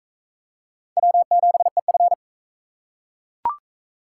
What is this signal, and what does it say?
Morse code, heard as a keyed tone, about 41 wpm, 700 Hz. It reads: W7EF